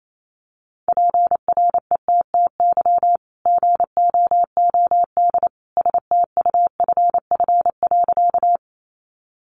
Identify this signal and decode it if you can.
Morse code, heard as a keyed tone, about 28 wpm, 700 Hz. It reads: PRETTY GOOD STUFF.
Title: Sentences from Top 1400 Words 28wpm